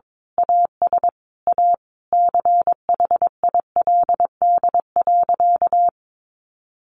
Morse code, heard as a keyed tone, about 22 wpm, 700 Hz. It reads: AS A CHILD.
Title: Sentences from Top 1500 Words 22wpm